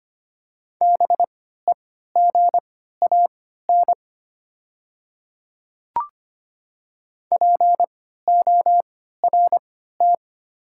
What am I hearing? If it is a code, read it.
Morse code, heard as a keyed tone, about 25 wpm, spaced as for 14 wpm, 700 Hz. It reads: BEGAN PORT
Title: 3x Character Spacing - Top 300 Words - 25wpm